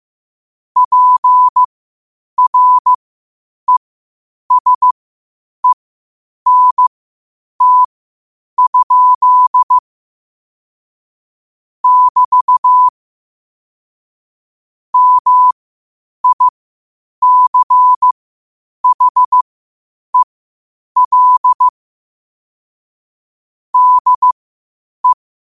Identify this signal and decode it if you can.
Morse code, heard as a keyed tone, about 15 wpm, spaced as for 8 wpm, 1000 Hz. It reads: PRESENT? <BT> MICHEL DE
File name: qotdfiles / 8wpm